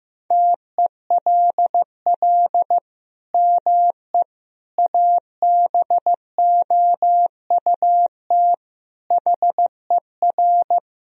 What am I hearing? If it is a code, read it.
Morse code, heard as a keyed tone, about 15 wpm, 700 Hz. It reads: TELL ME ABOUT HER
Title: Sentences from Top 1000 Words - Review Part 2 - 15wpm